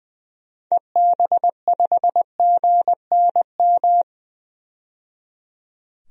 Morse code, heard as a keyed tone, about 21 wpm, 700 Hz. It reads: EB5GNM
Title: DXCC-100 Call Signs - 6 Characters - 20wpm